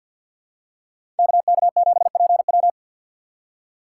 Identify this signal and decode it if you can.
Morse code, heard as a keyed tone, about 50 wpm, 700 Hz. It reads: KK7PW
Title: USA Call Signs - 5 Characters - 3x Character Spacing - 50wpm